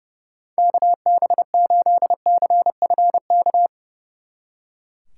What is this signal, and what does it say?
Morse code, heard as a keyed tone, about 37 wpm, 700 Hz. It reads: KB8CFK
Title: Speed Racing - USA 2x3 Call Signs 30wpm